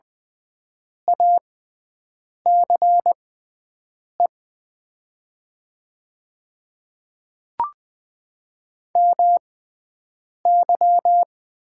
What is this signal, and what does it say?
Morse code, heard as a keyed tone, about 20 wpm, spaced as for 7 wpm, 700 Hz. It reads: ACE MY
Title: 6x Character Spacing - Top 500 Words - 20wpm